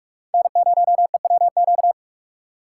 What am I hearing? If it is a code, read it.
Morse code, heard as a keyed tone, about 46 wpm, 700 Hz. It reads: N0EWQ
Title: USA Call Signs - 5 Characters - 3x Character Spacing - 45wpm